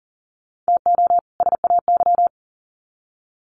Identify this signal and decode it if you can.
Morse code, heard as a keyed tone, about 38 wpm, 700 Hz. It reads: TO SAY
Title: Sets of 2 Words 40wpm